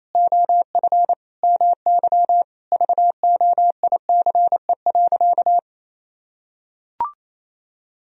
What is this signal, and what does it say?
Morse code, heard as a keyed tone, about 28 wpm, 700 Hz. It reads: OF MY VOICE.
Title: Sentences from Top 1000 Words - Review Part 4 - 28wpm